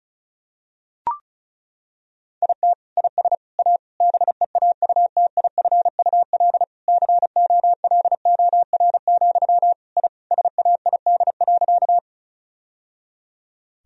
Morse code, heard as a keyed tone, about 36 wpm, 700 Hz. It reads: IT IS A BEAUTIFUL COLOR, I SAID.
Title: Sentences from Top 1000 Words - Review Part 2 - 35wpm